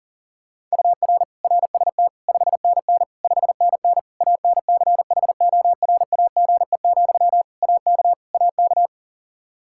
Morse code, heard as a keyed tone, about 40 wpm, 700 Hz. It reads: UR RST 5NN 5NN ANCHORAGE, AK AK
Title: Practice QSO - Exchange 1 - Beginning - 40wpm